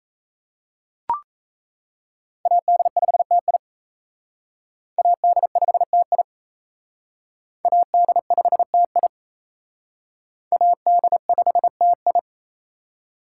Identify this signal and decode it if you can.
Morse code, heard as a keyed tone, about 34 wpm, 700 Hz. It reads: AD5TI AD5TI AD5TI AD5TI
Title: Speed Racing - USA 2x2 Call Signs 28wpm